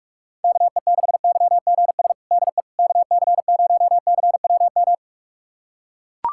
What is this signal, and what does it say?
Morse code, heard as a keyed tone, about 42 wpm, 700 Hz. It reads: KE6YGS DE KC0CWM
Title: USA Callsign DE Callsign 45wpm